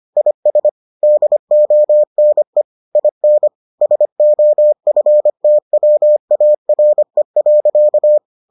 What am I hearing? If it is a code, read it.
Morse code, heard as a keyed tone, about 25 wpm, 600 Hz. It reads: IS DONE IN SOFTWARE.